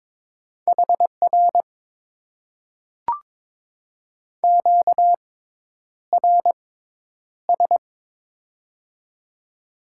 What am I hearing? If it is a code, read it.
Morse code, heard as a keyed tone, about 22 wpm, spaced as for 10 wpm, 700 Hz. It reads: HR QRS
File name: Top 90 QSO Elements - 6x Character Spacing - 22wpm